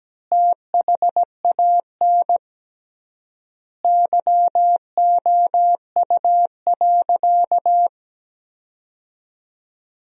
Morse code, heard as a keyed tone, about 17 wpm, 700 Hz. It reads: THAN YOU.
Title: 3x Word-Spacing - Sentences from Top 400 Words - 17wpm